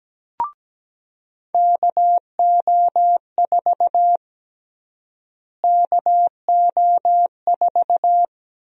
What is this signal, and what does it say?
Morse code, heard as a keyed tone, about 17 wpm, 700 Hz. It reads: KO4 KO4
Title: USA Call Sign Prefixes - Repeated 2x - 17wpm